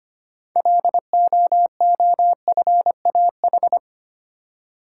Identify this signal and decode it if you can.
Morse code, heard as a keyed tone, about 25 wpm, 700 Hz. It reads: LOOFAH
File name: practice / 25wpm / Difficult Words with Hints - 25wpm